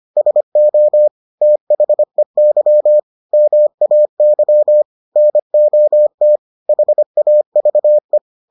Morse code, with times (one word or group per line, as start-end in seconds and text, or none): 0.2-1.1 SO
1.4-3.0 THEY
3.3-4.8 MAY
5.2-6.4 NOT
6.7-8.2 HAVE